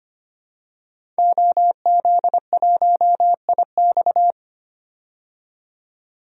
Morse code, heard as a keyed tone, about 25 wpm, 700 Hz. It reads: OZ1IX
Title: DXCC-100 Call Signs - 5 Characters - 25wpm